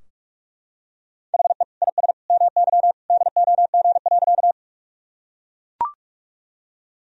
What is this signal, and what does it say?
Morse code, heard as a keyed tone, about 44 wpm, 700 Hz. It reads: HE IS MY DOG.
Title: Sentences from Top 1000 Words - Review Part 2 - 45wpm